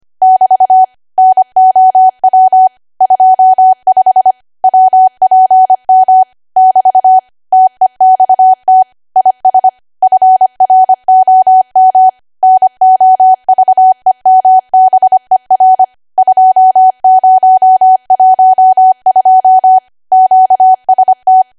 Morse code, heard as a keyed tone, about 25 wpm, 750 Hz. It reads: <BT> NOW 25 WPM <BT> TEXT IS FROM NOVEMBER 2012 QST